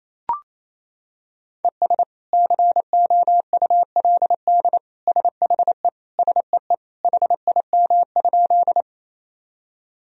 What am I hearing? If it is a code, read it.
Morse code, heard as a keyed tone, about 28 wpm, 700 Hz. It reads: ES COULD SHE SEE HIM?
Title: Sentences from Top 100 Words and Abbreviations 28wpm